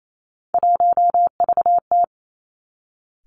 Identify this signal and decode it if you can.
Morse code, heard as a keyed tone, about 28 wpm, 700 Hz. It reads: 1VT
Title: Three Challenging Letter-Number Combinations 28wpm